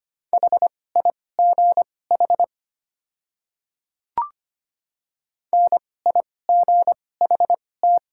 Morse code, heard as a keyed tone, about 25 wpm, spaced as for 18 wpm, 700 Hz. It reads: HIGH NIGHT
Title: 2x Character Spacing - Top 300 Words - 25wpm